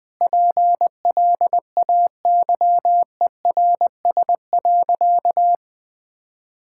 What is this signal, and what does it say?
Morse code, heard as a keyed tone, about 21 wpm, 700 Hz. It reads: PLAYERS.
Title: Sentences with 5-Word Phrases - Spoken Beginning - 20wpm